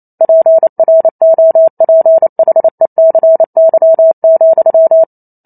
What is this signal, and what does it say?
Morse code, heard as a keyed tone, about 28 wpm, 650 Hz. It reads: PROPHECY,